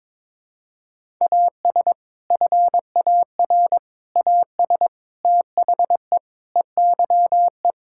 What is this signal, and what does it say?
Morse code, heard as a keyed tone, about 21 wpm, 700 Hz. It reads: AS FAR AS THE EYE
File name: practice / 22wpm / Sets of 7 Words - 22wpm